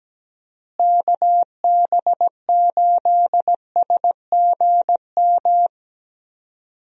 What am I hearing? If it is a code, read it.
Morse code, heard as a keyed tone, about 17 wpm, 700 Hz. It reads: KB8SGM
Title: USA Call Signs - 6 Characters - Repeated 3x - 17wpm